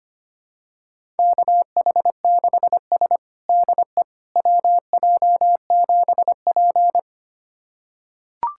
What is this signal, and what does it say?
Morse code, heard as a keyed tone, about 25 wpm, 700 Hz. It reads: KH6S DE WJ7P
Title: USA Callsign DE Callsign - 4 Characters - 25wpm